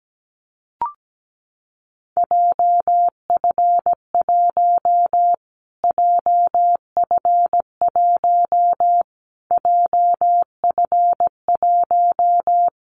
Morse code, heard as a keyed tone, about 17 wpm, 700 Hz. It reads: JF1 JF1 JF1
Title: Callsign Prefixes for Top 25 DXCC - Repeated 3x - 17wpm